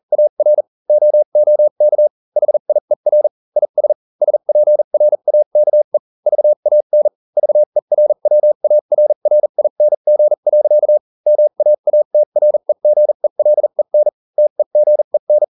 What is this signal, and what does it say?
Morse code, heard as a keyed tone, about 40 wpm, spaced as for 35 wpm, 600 Hz. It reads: AR OOK HIER IS SPRAKE VAN VERWARRING. MAATREGELEN TEGEN